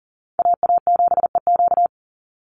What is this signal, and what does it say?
Morse code, heard as a keyed tone, about 40 wpm, 700 Hz. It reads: AA7EQ
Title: NA SOTA Chaser Callsigns 40wpm